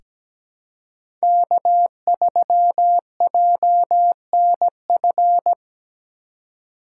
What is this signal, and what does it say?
Morse code, heard as a keyed tone, about 17 wpm, 700 Hz. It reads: K3JNF
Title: USA Call Signs - 5 Characters - 4x Character Spacing - 17wpm